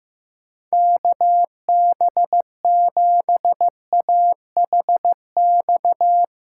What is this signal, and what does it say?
Morse code, heard as a keyed tone, about 15 wpm, 700 Hz. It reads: KB7AHX
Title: USA Call Signs - 8x Character Spacing - 15wpm